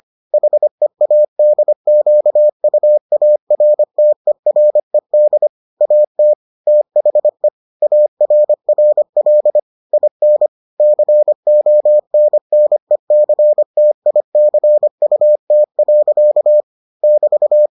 Morse code, heard as a keyed tone, about 25 wpm, 600 Hz. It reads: HEADQUARTERED AT THE ARRL IN CONNECTICUT. <BT>